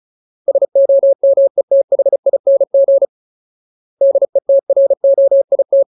Morse code, heard as a keyed tone, about 34 wpm, 550 Hz. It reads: SOMETHING DETROIT
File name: Ultimate Word List / 35WPM Ultimate